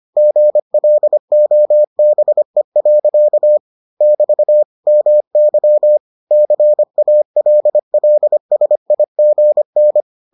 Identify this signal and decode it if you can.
Morse code, heard as a keyed tone, about 25 wpm, 600 Hz. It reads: GLOBE. <BT> MY CALLSIGN